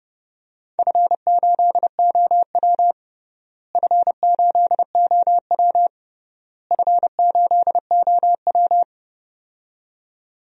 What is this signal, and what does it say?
Morse code, heard as a keyed tone, about 32 wpm, 700 Hz. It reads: F8OW F8OW F8OW
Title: DXCC-100 Call Signs - 4 Characters - Repeated 3x - 30wpm